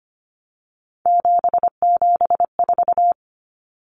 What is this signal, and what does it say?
Morse code, heard as a keyed tone, about 25 wpm, 700 Hz. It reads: 774
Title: Common Cut Numbers - 3-Digit 25wpm